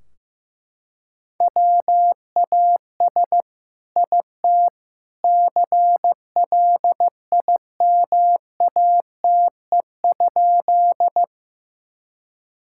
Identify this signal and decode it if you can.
Morse code, heard as a keyed tone, about 15 wpm, 700 Hz. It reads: WAS IT CLIMATE?
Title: Sentences from Top 1600 Words 15wpm